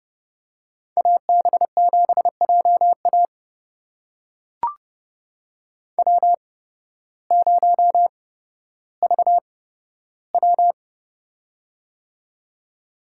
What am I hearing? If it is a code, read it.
Morse code, heard as a keyed tone, about 30 wpm, spaced as for 14 wpm, 700 Hz. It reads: AB7JA W0VW
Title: USA Call Signs - 8x Character Spacing - 30wpm